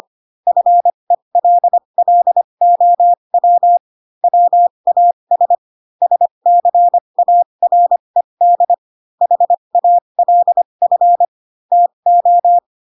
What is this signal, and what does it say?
Morse code, heard as a keyed tone, about 25 wpm, spaced as for 22 wpm, 700 Hz. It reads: FELLOW WAS SCARED HALF TO